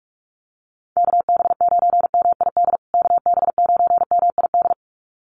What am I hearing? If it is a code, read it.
Morse code, heard as a keyed tone, about 44 wpm, 700 Hz. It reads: KB9MID KB9MID
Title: USA Call Signs Repeated 2x 45wpm